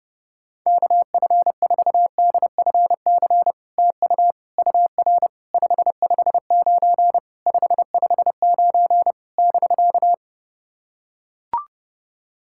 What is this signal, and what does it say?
Morse code, heard as a keyed tone, about 30 wpm, 700 Hz. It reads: KF4DFC TU UR 559 559 <BK>
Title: SOTA QSO Practice 30wpm